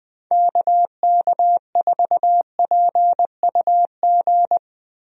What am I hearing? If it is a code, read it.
Morse code, heard as a keyed tone, about 20 wpm, 700 Hz. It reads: KK4PUG
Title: USA Call Signs - 6 Characters - Repeated 2x - 20wpm